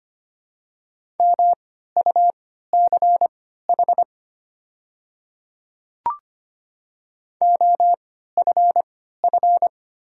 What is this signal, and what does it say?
Morse code, heard as a keyed tone, about 25 wpm, spaced as for 14 wpm, 690 Hz. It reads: MUCH OFF
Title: 3x Character Spacing - Top 300 Words - 25wpm